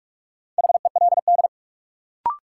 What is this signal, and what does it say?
Morse code, heard as a keyed tone, about 44 wpm, 700 Hz. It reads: HELD
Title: Sentences from Top 1600 Words - New Word Drill - 45wpm